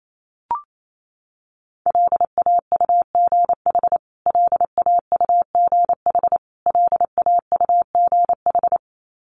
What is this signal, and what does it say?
Morse code, heard as a keyed tone, about 28 wpm, 700 Hz. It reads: LAUGH LAUGH LAUGH
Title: Top 500 Words Repeated 28wpm